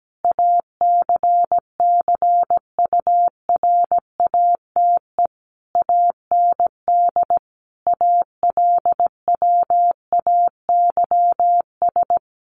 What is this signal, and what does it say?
Morse code, heard as a keyed tone, about 17 wpm, 700 Hz. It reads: ACCURATE AND ALWAYS
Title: Sentences from Top 1200 Words 17wpm